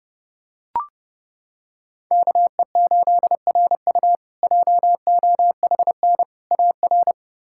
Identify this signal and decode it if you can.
Morse code, heard as a keyed tone, about 30 wpm, 700 Hz. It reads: KE8RU JOHN AR
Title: K1USN SST QSO Practice - 30wpm